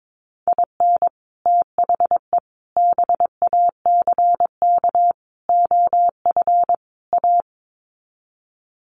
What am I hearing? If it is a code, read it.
Morse code, heard as a keyed tone, about 22 wpm, 700 Hz. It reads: IN THE BACK OF A
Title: Sets of 5 Words Encore2 22wpm